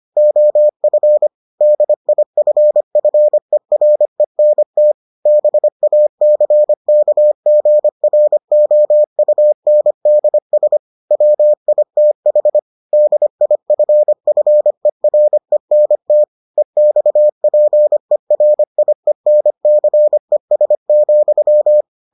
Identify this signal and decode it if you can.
Morse code, heard as a keyed tone, about 25 wpm, 600 Hz. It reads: OF DIFFERENT BACKGROUNDS WITH DIFFERENT EXPERIENCES,